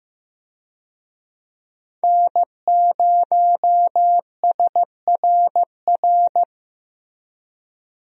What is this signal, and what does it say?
Morse code, heard as a keyed tone, about 15 wpm, 700 Hz. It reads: N0SRR
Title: USA Call Signs - 5 Characters - Repeated 2x - 15wpm